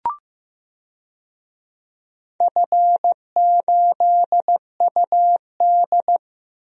Morse code, heard as a keyed tone, about 15 wpm, 700 Hz. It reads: F8UD
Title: DXCC-100 Call Signs - 4 Characters - 15wpm